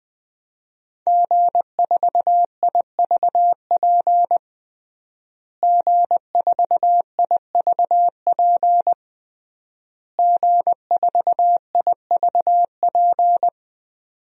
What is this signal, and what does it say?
Morse code, heard as a keyed tone, about 20 wpm, 700 Hz. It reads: G4IVP G4IVP G4IVP
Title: DXCC-100 Call Signs - 5 Characters - Repeated 3x - 20wpm